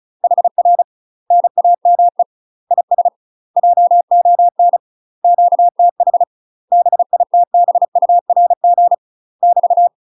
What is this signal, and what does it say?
Morse code, heard as a keyed tone, about 34 wpm, 700 Hz. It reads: HR NAME IS JON QTH BITBURG <BT>
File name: UK MASTER QSOs TRAINING / 35WPM / UK MASTER QSOs 0004